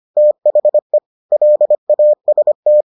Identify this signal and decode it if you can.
Morse code, heard as a keyed tone, about 25 wpm, 600 Hz. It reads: THE LAST